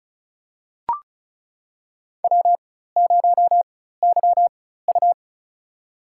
Morse code, heard as a keyed tone, about 36 wpm, spaced as for 16 wpm, 700 Hz. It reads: W0YU
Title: USA Call Signs - 4x Character Spacing - 35wpm